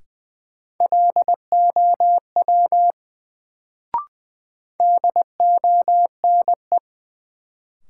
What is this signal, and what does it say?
Morse code, heard as a keyed tone, about 20 wpm, 700 Hz. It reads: LOW DONE
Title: Sentences from Top 500 Words - New Word Drill - 20wpm